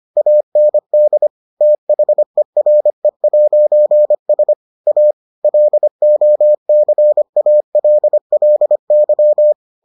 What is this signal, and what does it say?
Morse code, heard as a keyed tone, about 25 wpm, 600 Hz. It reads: AND THERE'S A LOCALLY